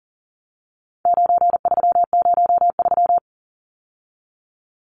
Morse code, heard as a keyed tone, about 40 wpm, 700 Hz. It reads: 9303
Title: Numbers - 4-Digit 40wpm